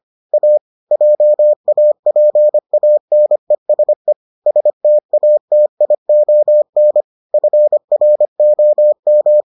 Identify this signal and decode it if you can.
Morse code, heard as a keyed tone, about 25 wpm, 600 Hz. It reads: A JAPANESE STATION FROM